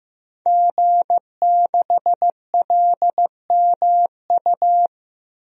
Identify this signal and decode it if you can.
Morse code, heard as a keyed tone, about 15 wpm, 700 Hz. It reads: G6LMU
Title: DXCC-100 Call Signs - 5 Characters - Repeated 3x - 15wpm